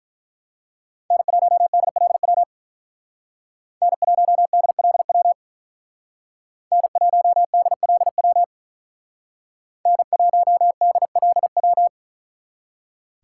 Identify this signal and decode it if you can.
Morse code, heard as a keyed tone, about 44 wpm, 700 Hz. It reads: N1DLW N1DLW N1DLW N1DLW
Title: Speed Racing - USA 1x3 Call Signs 35wpm